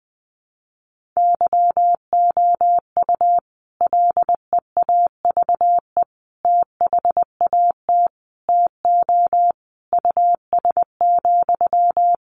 Morse code, heard as a keyed tone, about 20 wpm, 700 Hz. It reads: YOU LEAVE THAT TO US,
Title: Sentences from Top 1400 Words 20wpm